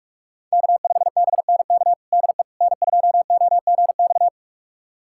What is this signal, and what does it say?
Morse code, heard as a keyed tone, about 41 wpm, 700 Hz. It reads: K5BNK DE N2OGX
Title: USA Callsign DE Callsign - 5 Characters - 45wpm